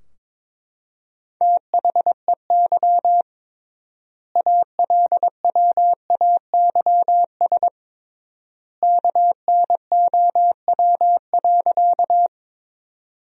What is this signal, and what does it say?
Morse code, heard as a keyed tone, about 22 wpm, 700 Hz. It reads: THEY ALWAYS KNOW.
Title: 3x Word-Spacing - Sentences from Top 300 Words - 22wpm